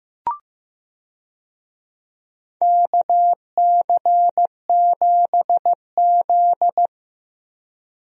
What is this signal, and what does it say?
Morse code, heard as a keyed tone, about 15 wpm, 700 Hz. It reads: KC7Z
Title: USA Call Signs - 4 Characters - Repeated 2x - 15wpm